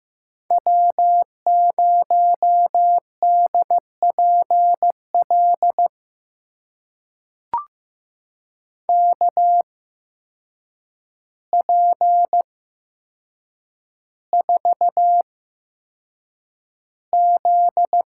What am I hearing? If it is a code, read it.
Morse code, heard as a keyed tone, about 15 wpm, spaced as for 7 wpm, 700 Hz. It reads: W0DPL KP4Z
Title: USA Call Signs - 5 Characters - 8x Character Spacing - 15wpm